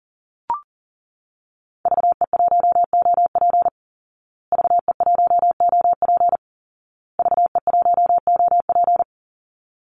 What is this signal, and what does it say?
Morse code, heard as a keyed tone, about 40 wpm, 700 Hz. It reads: VE1OP VE1OP VE1OP
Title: Call Signs Repeated 2x - 40wpm